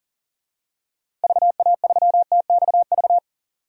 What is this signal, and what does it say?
Morse code, heard as a keyed tone, about 40 wpm, 700 Hz. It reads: VA3TXV
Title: DXCC-100 Call Signs - 6 Characters - Repeated 2x - 40wpm